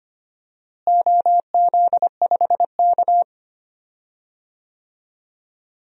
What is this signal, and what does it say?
Morse code, heard as a keyed tone, about 25 wpm, 700 Hz. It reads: OZ5K
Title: DXCC-100 Call Signs - 4 Characters - Repeated 2x - 25wpm